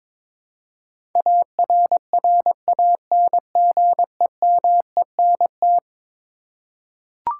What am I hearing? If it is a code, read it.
Morse code, heard as a keyed tone, about 22 wpm, 700 Hz. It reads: ARRANGEMENT
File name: practice / 22wpm / Eleven Letter Words - 22wpm